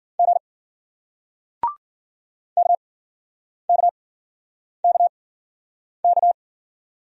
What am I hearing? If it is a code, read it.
Morse code, heard as a keyed tone, about 50 wpm, spaced as for 9 wpm, 700 Hz. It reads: G KKKK